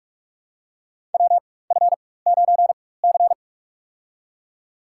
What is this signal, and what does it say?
Morse code, heard as a keyed tone, about 46 wpm, spaced as for 21 wpm, 700 Hz. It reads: WF9C
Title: USA Call Signs - 4 Characters - 4x Character Spacing - 45wpm